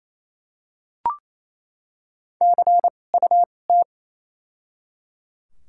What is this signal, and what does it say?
Morse code, heard as a keyed tone, about 28 wpm, spaced as for 20 wpm, 700 Hz. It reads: CUT
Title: 2x Character Spacing - Top 500 Words - 28wpm